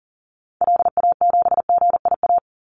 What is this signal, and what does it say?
Morse code, heard as a keyed tone, about 40 wpm, 700 Hz. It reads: LA7GIA
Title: Call Signs Repeated 2x - 40wpm